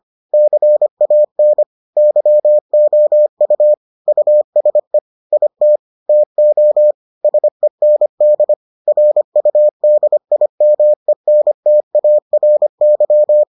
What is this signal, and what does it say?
Morse code, heard as a keyed tone, about 25 wpm, 600 Hz. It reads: CAN YOU USE IT TO SEND RUDIMENTARY